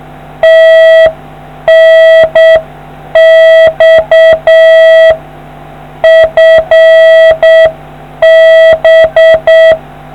Morse code, heard as a keyed tone, about 7 wpm, 650 Hz. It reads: TNX FB